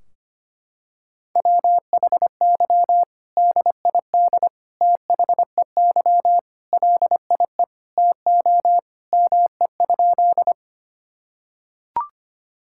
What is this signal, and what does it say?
Morse code, heard as a keyed tone, about 25 wpm, 700 Hz. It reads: WHY DID THEY LIE TO ME?
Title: Sentences from Top 1600 Words 25wpm